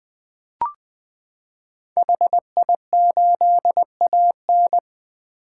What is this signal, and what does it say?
Morse code, heard as a keyed tone, about 20 wpm, 700 Hz. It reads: HI8AN
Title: 2023 POTA Activator Call Signs - 20wpm